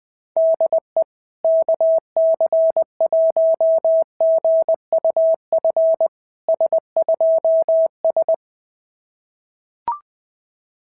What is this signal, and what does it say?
Morse code, heard as a keyed tone, about 20 wpm, 660 Hz. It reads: DE KC1GUF S2S
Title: SOTA QSO Practice 20wpm